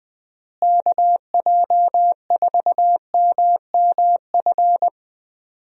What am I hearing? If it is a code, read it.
Morse code, heard as a keyed tone, about 20 wpm, 700 Hz. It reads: KJ4MMF
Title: USA Call Signs - 6 Characters - Repeated 3x - 20wpm